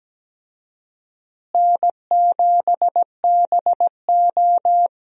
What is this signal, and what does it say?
Morse code, heard as a keyed tone, about 17 wpm, 700 Hz. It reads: N7BO
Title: USA Call Signs - 4 Characters - Repeated 2x - 17wpm